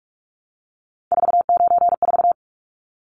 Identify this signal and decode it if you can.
Morse code, heard as a keyed tone, about 44 wpm, 700 Hz. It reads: 494
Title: Numbers - 3-Digit 45wpm